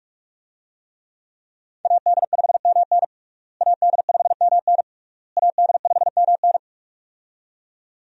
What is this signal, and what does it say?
Morse code, heard as a keyed tone, about 44 wpm, 700 Hz. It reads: AD5MN AD5MN AD5MN
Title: USA Call Signs - 5 Characters - Repeated 3x - 45wpm